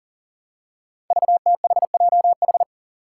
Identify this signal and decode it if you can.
Morse code, heard as a keyed tone, about 39 wpm, 700 Hz. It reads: VTHJH
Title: Five Challenging Letter-Number Combinations 40wpm